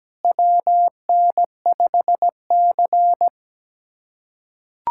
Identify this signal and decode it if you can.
Morse code, heard as a keyed tone, about 17 wpm, 700 Hz. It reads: WN5C
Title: USA Callsign DE Callsign - 4 Characters - 17wpm